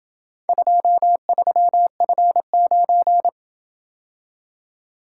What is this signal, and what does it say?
Morse code, heard as a keyed tone, about 26 wpm, 700 Hz. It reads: 23F9